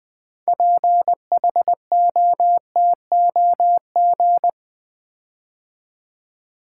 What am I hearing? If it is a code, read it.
Morse code, heard as a keyed tone, about 20 wpm, 700 Hz. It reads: PHOTOG
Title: Build A Word - 6 Letters At A Time - 12-Letter Words - 20wpm